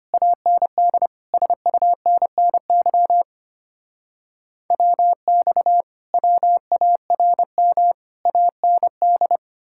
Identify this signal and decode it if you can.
Morse code, heard as a keyed tone, about 33 wpm, 700 Hz. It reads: AND SUNNY WX WARM AND